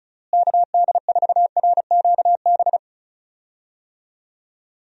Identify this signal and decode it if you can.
Morse code, heard as a keyed tone, about 36 wpm, 700 Hz. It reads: KD4RQB
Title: USA Call Signs - 6 Characters - Repeated 3x - 35wpm